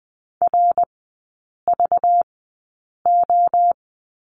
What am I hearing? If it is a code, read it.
Morse code, heard as a keyed tone, about 20 wpm, spaced as for 8 wpm, 700 Hz. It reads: RVO
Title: 191 Lesson 31e - Letter v - Three Characters 20wpm